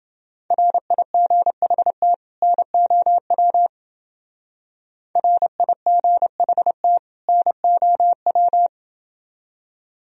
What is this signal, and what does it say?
Morse code, heard as a keyed tone, about 28 wpm, 700 Hz. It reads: RIGHT NOW RIGHT NOW